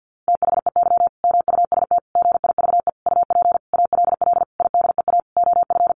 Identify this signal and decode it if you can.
Morse code, heard as a keyed tone, about 49 wpm, 700 Hz. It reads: THEY MUST GIVE UP ALL IDEA OF
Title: Sentences from Top 1100 Words 50wpm